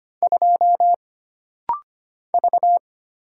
Morse code, heard as a keyed tone, about 25 wpm, 700 Hz. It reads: 2 V
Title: Single Letter-Number Alt - Rapid-Fire 25wpm